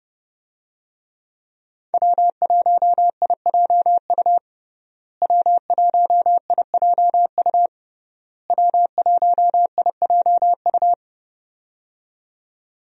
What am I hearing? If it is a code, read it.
Morse code, heard as a keyed tone, about 30 wpm, 700 Hz. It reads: W1IJU W1IJU W1IJU
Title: USA Call Signs - 5 Characters - Repeated 3x - 30wpm